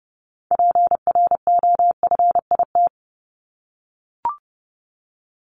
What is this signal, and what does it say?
Morse code, heard as a keyed tone, about 30 wpm, 700 Hz. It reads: PROFIT
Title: Top 751-1000 Words 30wpm